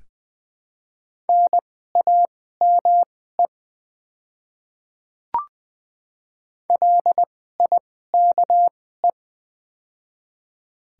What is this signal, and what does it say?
Morse code, heard as a keyed tone, about 20 wpm, spaced as for 15 wpm, 700 Hz. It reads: NAME LIKE
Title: 2x Character Spacing - Top 300 Words - 20wpm